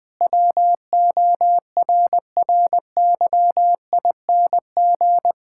Morse code, heard as a keyed tone, about 19 wpm, 700 Hz. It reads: WORRYING
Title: Sentences with 8-Word Phrases - Spoken Beginning - 20wpm